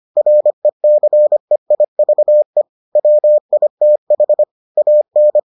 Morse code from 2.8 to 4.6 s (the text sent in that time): WITH